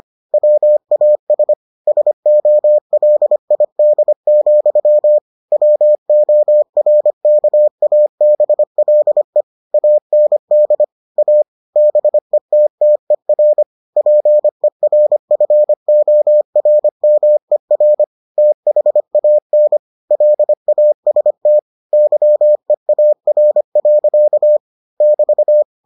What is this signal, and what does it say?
Morse code, heard as a keyed tone, about 25 wpm, 600 Hz. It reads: WAS SOLID, WORKABLE AND A BETTER PERFORMER THAN LAST YEAR. <BT>